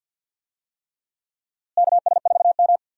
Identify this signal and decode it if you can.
Morse code, heard as a keyed tone, about 52 wpm, 700 Hz. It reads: KS4M
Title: USA Call Signs - 4 Characters - Repeated 3x - 50wpm